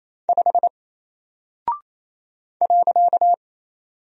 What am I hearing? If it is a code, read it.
Morse code, heard as a keyed tone, about 28 wpm, 700 Hz. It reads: 5 .
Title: Single Character (Letter:Number:Punctuation:Prosign) 28wpm